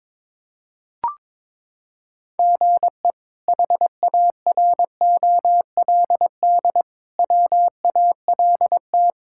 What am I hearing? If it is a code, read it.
Morse code, heard as a keyed tone, about 22 wpm, 700 Hz. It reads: GE HAROLD WALT
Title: K1USN SST QSO Practice - 22wpm